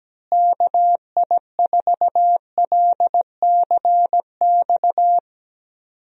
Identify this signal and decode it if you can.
Morse code, heard as a keyed tone, about 17 wpm, 700 Hz. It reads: KI4LCX
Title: USA Call Signs - 6 Characters - Repeated 2x - 17wpm